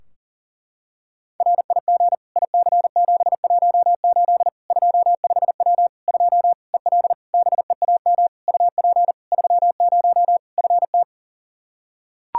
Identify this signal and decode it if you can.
Morse code, heard as a keyed tone, about 40 wpm, 700 Hz. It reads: RIG IC718 25W 2 EL BEAM UP 30 FT
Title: Practice QSO - Exchange 2 - Info - 40wpm